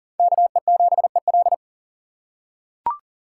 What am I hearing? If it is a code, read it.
Morse code, heard as a keyed tone, about 37 wpm, 700 Hz. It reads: KE7EL
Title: USA Call Signs - 5 Characters - 6x Character Spacing - 40wpm